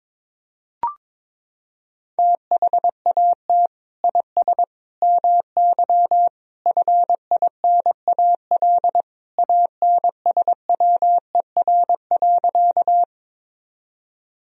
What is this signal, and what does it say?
Morse code, heard as a keyed tone, about 22 wpm, 700 Hz. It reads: THAT IS MY FINAL ANSWER.
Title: Sentences from Top 1000 Words - Review Part 2 - 22wpm